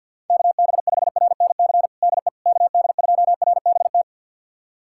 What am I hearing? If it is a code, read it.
Morse code, heard as a keyed tone, about 50 wpm, 700 Hz. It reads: KB5RNC DE KD2RBT
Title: USA Callsign DE Callsign - 6 Characters - 50wpm